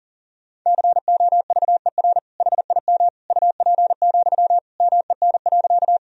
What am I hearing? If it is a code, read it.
Morse code, heard as a keyed tone, about 40 wpm, 700 Hz. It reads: COVER HIM UP, MEN.
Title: Sentences from Top 1000 Words - Review Part 1 - 40wpm